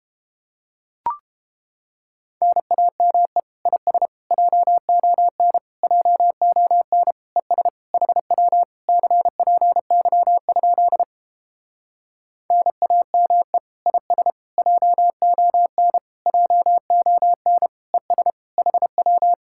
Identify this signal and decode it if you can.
Morse code, heard as a keyed tone, about 31 wpm, 700 Hz. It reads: NAME IS JON JON ES HW CPY? NAME IS JON JON ES HW